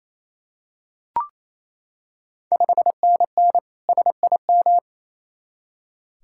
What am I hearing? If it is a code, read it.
Morse code, heard as a keyed tone, about 28 wpm, 700 Hz. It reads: 5NN SIM
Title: Ontario QSO Party Exchange 28wpm